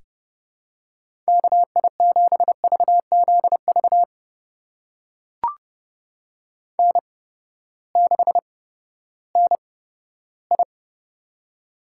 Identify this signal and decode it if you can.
Morse code, heard as a keyed tone, about 30 wpm, spaced as for 15 wpm, 700 Hz. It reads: KI7VZV N6NI
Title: USA Call Signs - 8x Character Spacing - 30wpm